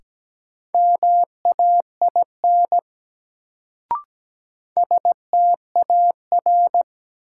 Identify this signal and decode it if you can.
Morse code, heard as a keyed tone, about 17 wpm, 700 Hz. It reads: MAIN STAR
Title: Sentences from Top 1000 Words - Review Part 2 - New Word Drill - 17wpm